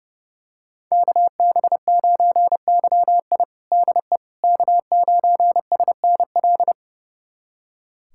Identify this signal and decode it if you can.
Morse code, heard as a keyed tone, about 32 wpm, 700 Hz. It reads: KB9YI DE K9SNL
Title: USA Callsign DE Callsign - 5 Characters - 30wpm